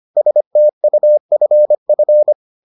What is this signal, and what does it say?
Morse code, heard as a keyed tone, about 25 wpm, 600 Hz. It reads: STUFF